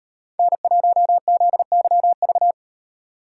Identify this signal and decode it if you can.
Morse code, heard as a keyed tone, about 36 wpm, 700 Hz. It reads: N1ZYV